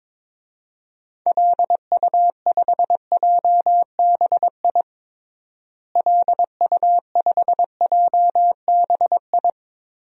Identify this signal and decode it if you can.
Morse code, heard as a keyed tone, about 22 wpm, 700 Hz. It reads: LU5JBI LU5JBI
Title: DXCC-100 Call Signs - 6 Characters - Repeated 3x - 22wpm